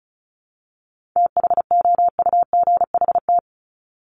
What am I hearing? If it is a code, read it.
Morse code, heard as a keyed tone, about 34 wpm, 700 Hz. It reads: THOUGHT
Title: Top 500 Words 35wpm